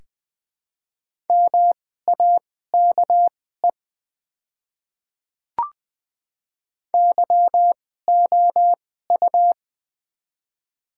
Morse code, heard as a keyed tone, about 20 wpm, spaced as for 14 wpm, 700 Hz. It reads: MAKE YOU
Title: 2x Character Spacing - Top 100 Words - 20wpm